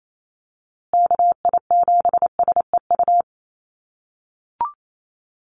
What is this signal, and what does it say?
Morse code, heard as a keyed tone, about 28 wpm, 700 Hz. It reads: KI7SEU
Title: USA Call Signs Repeated 3x 28wpm